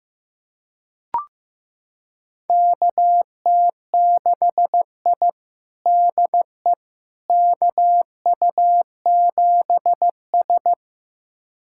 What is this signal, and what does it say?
Morse code, heard as a keyed tone, about 15 wpm, 700 Hz. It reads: KT6I DE KU7S
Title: USA Callsign DE Callsign - 4 Characters - 15wpm